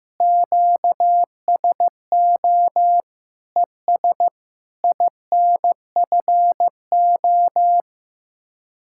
Morse code, heard as a keyed tone, about 15 wpm, 700 Hz. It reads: QSO ES INFO
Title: Speed-Racing Common QSO Phrases Encore - 15wpm